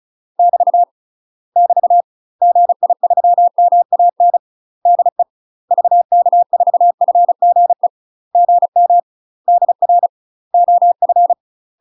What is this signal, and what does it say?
Morse code, heard as a keyed tone, about 34 wpm, 700 Hz. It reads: <BT> <BT> GI3MAN DE VK4FGE GM DR OF
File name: UK MASTER QSOs TRAINING / 35WPM / UK MASTER QSOs 0001